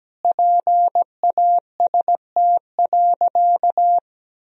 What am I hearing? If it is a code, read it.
Morse code, heard as a keyed tone, about 17 wpm, 700 Hz. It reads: PAST.
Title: Sentences from Top 1000 Words - Review Part 3 - 17wpm